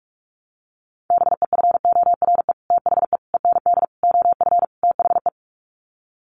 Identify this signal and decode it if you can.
Morse code, heard as a keyed tone, about 44 wpm, 700 Hz. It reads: BEFORE THE END OF THE
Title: Sets of 5 Words Encore 45wpm